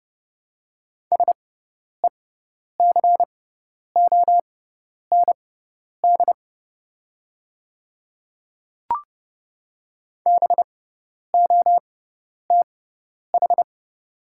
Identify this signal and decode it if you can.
Morse code, heard as a keyed tone, about 30 wpm, spaced as for 10 wpm, 700 Hz. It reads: SECOND BOTH
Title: 6x Character Spacing - Top 300 Words - 30wpm